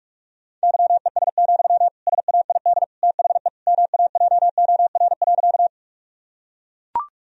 Morse code, heard as a keyed tone, about 41 wpm, 700 Hz. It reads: YES, SAID THE MAJOR.
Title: Sentences from Top 1000 Words - Review Part 4 - 45wpm